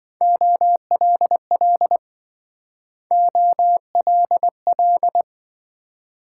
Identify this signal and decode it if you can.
Morse code, heard as a keyed tone, about 26 wpm, 700 Hz. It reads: OLL OLL